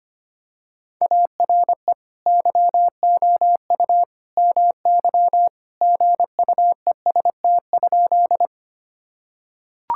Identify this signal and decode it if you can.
Morse code, heard as a keyed tone, about 25 wpm, 700 Hz. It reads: ARE YOU MY GUEST?
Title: Sentences from Top 1600 Words 25wpm